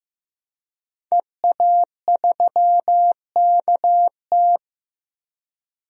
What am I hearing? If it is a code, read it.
Morse code, heard as a keyed tone, about 15 wpm, 700 Hz. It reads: EA3KT
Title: DXCC-100 Call Signs - 5 Characters - 15wpm